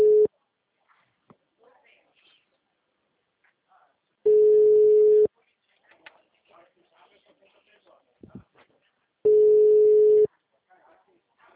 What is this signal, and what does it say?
Ringback tone (Europe, 425 Hz), with speech.
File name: out- -2018